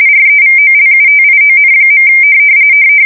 RTTY, about 45.45 baud. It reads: REVOLUTION.
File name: fsk170 / morkrum